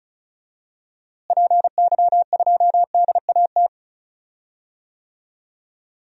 Morse code, heard as a keyed tone, about 34 wpm, 700 Hz. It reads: PY2DAT
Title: DXCC-100 Call Signs - 6 Characters - Repeated 3x - 35wpm